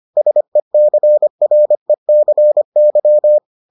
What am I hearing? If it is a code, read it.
Morse code, heard as a keyed tone, about 25 wpm, 600 Hz. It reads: SECRECY